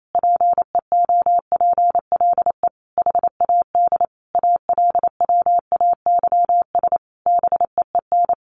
Morse code, heard as a keyed tone, about 28 wpm, 700 Hz. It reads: PEOPLE HAD ALWAYS BEEN
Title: Sentences from Top 1400 Words 28wpm